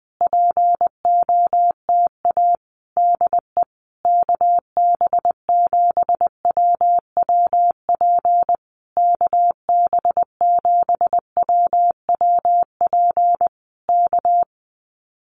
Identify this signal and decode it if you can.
Morse code, heard as a keyed tone, about 20 wpm, 700 Hz. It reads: POTA DE KB7WWP KB7WWP K
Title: Common QSO Phrases Encore 20wpm